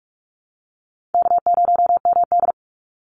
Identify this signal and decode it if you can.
Morse code, heard as a keyed tone, about 44 wpm, 700 Hz. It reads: K0MD
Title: Easy Call Signs Repeated 2x - 45wpm